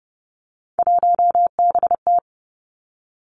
Morse code, heard as a keyed tone, about 30 wpm, 700 Hz. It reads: 1BT
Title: Three Challenging Letter-Number Combinations 30wpm